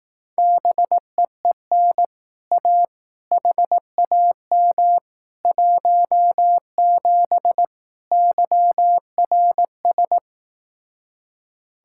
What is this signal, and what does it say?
Morse code, heard as a keyed tone, about 18 wpm, 700 Hz. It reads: BEEN A HAM 17 YRS
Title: Speed-Racing Common QSO Phrases Encore - 15wpm